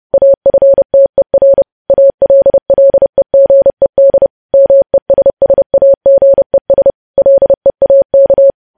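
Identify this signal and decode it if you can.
Morse code, heard as a keyed tone, about 30 wpm, 550 Hz. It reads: AFTER ALLEGED MESSAGES LEAK
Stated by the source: Morse code sending of news headlines at 30 wpm.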